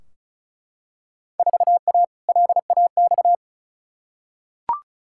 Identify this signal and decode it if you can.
Morse code, heard as a keyed tone, about 34 wpm, 700 Hz. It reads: 4A LAX
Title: Field Day Exchanges - 35wpm